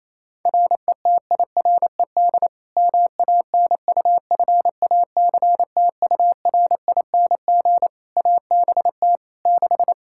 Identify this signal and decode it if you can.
Morse code, heard as a keyed tone, about 28 wpm, 700 Hz. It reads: RETIRED MANUFACTURING ABT 6
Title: Practice QSO - Exchange 3 - More Info - 28wpm